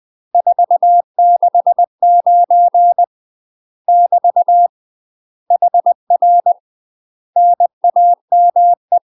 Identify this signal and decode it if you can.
Morse code, heard as a keyed tone, about 20 wpm, 700 Hz. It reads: 469 <BT> HR NAME
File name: UK MASTER QSOs TRAINING / 20WPM / UK MASTER QSOs 0018